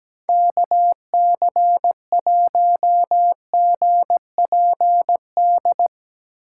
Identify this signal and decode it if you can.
Morse code, heard as a keyed tone, about 18 wpm, 700 Hz. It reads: KC1GPD
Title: Speed Racing - USA 2x3 Call Signs 17wpm